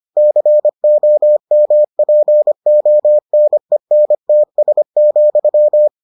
Morse code, heard as a keyed tone, about 25 wpm, 600 Hz. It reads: COMPONENTS,